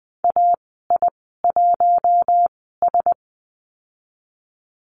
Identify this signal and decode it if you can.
Morse code, heard as a keyed tone, about 20 wpm, spaced as for 14 wpm, 700 Hz. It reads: AI1S
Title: 055 Lesson 10g - Letter r - Call Signs 20wpm